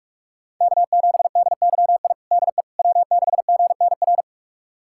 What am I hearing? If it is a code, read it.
Morse code, heard as a keyed tone, about 46 wpm, 700 Hz. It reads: K7DYI DE W6GNR
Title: USA Callsign DE Callsign - 5 Characters - 45wpm